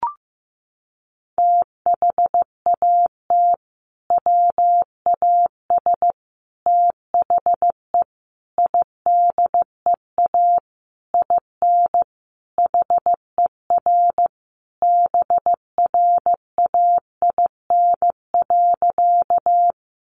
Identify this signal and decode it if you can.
Morse code, heard as a keyed tone, about 15 wpm, 700 Hz. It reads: THAT WAS THE IDEA IN HER BRAIN.
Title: Sentences from Top 1100 Words 15wpm